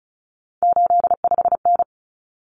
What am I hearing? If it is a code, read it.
Morse code, heard as a keyed tone, about 34 wpm, 700 Hz. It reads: 85N
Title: Common Cut Numbers - 3-Digit 35wpm